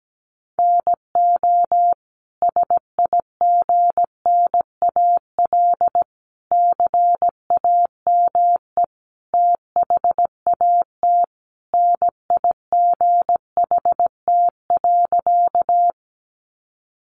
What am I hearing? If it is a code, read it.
Morse code, heard as a keyed tone, about 17 wpm, 700 Hz. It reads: NO SIGNAL CAME THAT NIGHT.
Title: Sentences from Top 1400 Words 17wpm